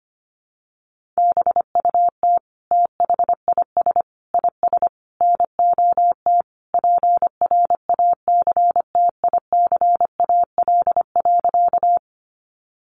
Morse code, heard as a keyed tone, about 25 wpm, 700 Hz. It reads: BUT THIS IS NOT PRACTICAL.
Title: Sentences from Top 1400 Words 25wpm